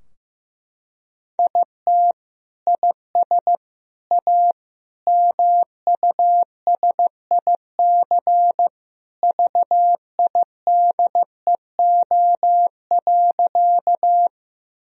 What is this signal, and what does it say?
Morse code, heard as a keyed tone, about 15 wpm, 700 Hz. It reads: IT IS A MUSIC VIDEO.
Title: Sentences from Top 1000 Words - Review Part 4 - 15wpm